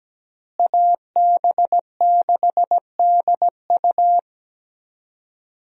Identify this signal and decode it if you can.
Morse code, heard as a keyed tone, about 17 wpm, 700 Hz. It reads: AB6DU
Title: USA Call Signs - 5 Characters - Repeated 2x - 17wpm